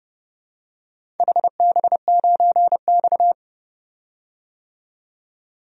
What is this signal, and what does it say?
Morse code, heard as a keyed tone, about 30 wpm, 700 Hz. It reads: HB9X
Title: DXCC-100 Call Signs - 4 Characters - 30wpm